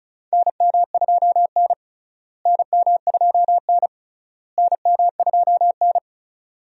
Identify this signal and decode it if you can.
Morse code, heard as a keyed tone, about 36 wpm, 700 Hz. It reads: NM2N NM2N NM2N
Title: USA Call Signs - 4 Characters - Repeated 3x - 35wpm